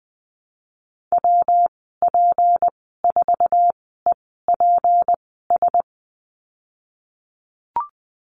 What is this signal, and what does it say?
Morse code, heard as a keyed tone, about 20 wpm, spaced as for 15 wpm, 700 Hz. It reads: WP4EPS
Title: 179 Lesson 29g - Slash - Call Signs 20wpm